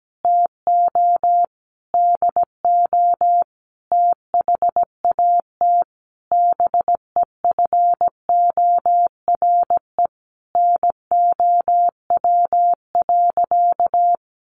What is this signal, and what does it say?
Morse code, heard as a keyed tone, about 17 wpm, 700 Hz. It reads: TO DO THAT BEFORE NOW.
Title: Sentences from Top 1300 Words 17wpm